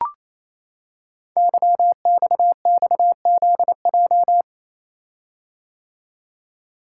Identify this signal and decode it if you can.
Morse code, heard as a keyed tone, about 29 wpm, 700 Hz. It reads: YXXZJ
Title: Code Groups - 5 Letters - 28wpm